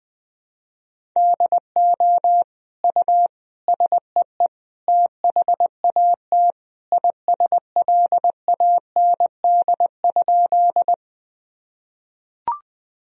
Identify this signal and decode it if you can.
Morse code, heard as a keyed tone, about 20 wpm, 700 Hz. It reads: DO U SEE THAT ISLAND?
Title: Sentences from Top 500 Words and Abbreviations 20wpm